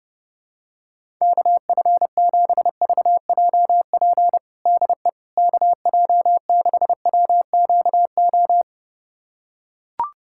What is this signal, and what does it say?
Morse code, heard as a keyed tone, about 30 wpm, 700 Hz. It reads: KF7VJP DE KJ6WQO
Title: USA Callsign DE Callsign 30wpm